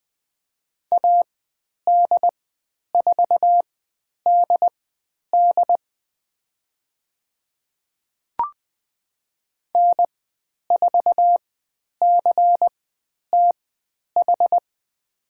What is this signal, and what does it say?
Morse code, heard as a keyed tone, about 20 wpm, spaced as for 10 wpm, 700 Hz. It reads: AD4DD N4CTH
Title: 095 Lesson 16g - Letter c - Call Signs 20wpm